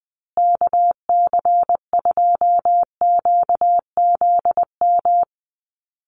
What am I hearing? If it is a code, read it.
Morse code, heard as a keyed tone, about 20 wpm, 700 Hz. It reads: KC2QZM
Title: USA Call Signs Repeated 2x 20wpm